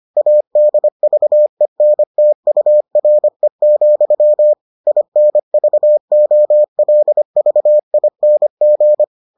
Morse code, heard as a keyed tone, about 25 wpm, 600 Hz. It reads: ADVENTURE, INVOLVING